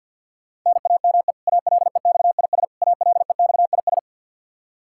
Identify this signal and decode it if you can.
Morse code, heard as a keyed tone, about 49 wpm, 700 Hz. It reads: NAME ALEXIS ALEXIS
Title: K1USN SST QSO Practice 50wpm